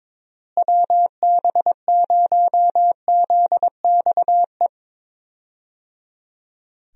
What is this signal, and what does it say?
Morse code, heard as a keyed tone, about 22 wpm, 700 Hz. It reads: WB0ZXE
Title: USA Call Signs - 6 Characters - Repeated 3x - 22wpm